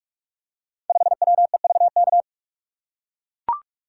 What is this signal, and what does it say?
Morse code, heard as a keyed tone, about 44 wpm, 700 Hz. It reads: 5WEVK
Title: Code Groups - 5 Letter:Numbers - 45wpm